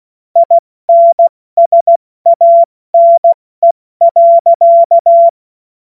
Morse code, heard as a keyed tone, about 16 wpm, 680 Hz. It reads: INSANE.
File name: Morsekurs / Witze / Witz 23 Januar 2024 70 F80